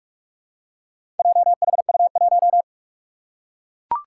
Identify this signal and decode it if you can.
Morse code, heard as a keyed tone, about 41 wpm, 700 Hz. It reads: JHU1
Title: Four Challenging Letter-Number Combinations 45wpm